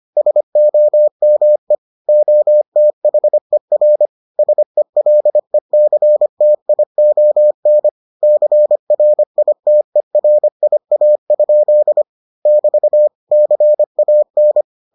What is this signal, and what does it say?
Morse code, heard as a keyed tone, about 25 wpm, 600 Hz. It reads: SOME OTHER SELECTION CRITERIA? <BT> CAN